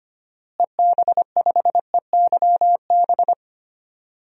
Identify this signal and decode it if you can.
Morse code, heard as a keyed tone, about 25 wpm, 700 Hz. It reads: EB5EYB
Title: DXCC-100 Call Signs - 6 Characters - Repeated 2x - 25wpm